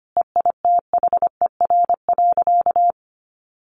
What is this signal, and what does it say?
Morse code, heard as a keyed tone, about 25 wpm, 700 Hz. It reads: EITHER.
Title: Sentences from Top 1200 Words 25wpm